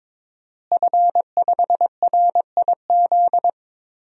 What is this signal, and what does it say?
Morse code, heard as a keyed tone, about 22 wpm, 700 Hz. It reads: F5RIZ
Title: DXCC-100 Call Signs - 5 Characters - Repeated 3x - 22wpm